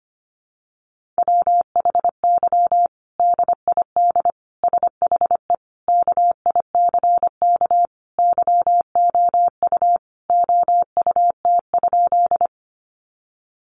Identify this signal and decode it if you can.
Morse code, heard as a keyed tone, about 25 wpm, 700 Hz. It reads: WHY DID SHE KICK YOU OUT?
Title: Sentences from Top 1300 Words 25wpm